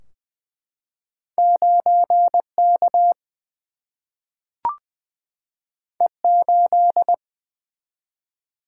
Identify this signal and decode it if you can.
Morse code, heard as a keyed tone, about 20 wpm, 700 Hz. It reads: 9 K E 8
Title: Single-Letter Single-Number 20wpm